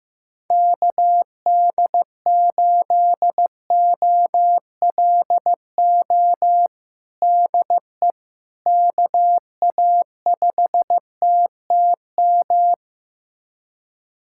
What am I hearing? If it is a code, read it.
Morse code, heard as a keyed tone, about 15 wpm, 700 Hz. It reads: KD8OLO DE KA5TTM
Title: USA Callsign DE Callsign 15wpm